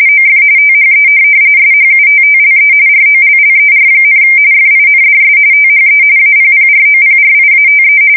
RTTY, about 45.45 baud. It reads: OF THE SHUTTLE ROLLER, FINDS THE APPROPRIATE
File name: fsk170 / morkrum